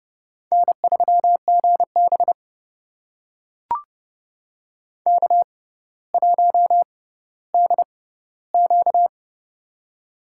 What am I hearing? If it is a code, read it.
Morse code, heard as a keyed tone, about 30 wpm, spaced as for 15 wpm, 700 Hz. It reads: N3GB K1DQ
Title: USA Call Signs - 4 Characters - 6x Character Spacing - 30wpm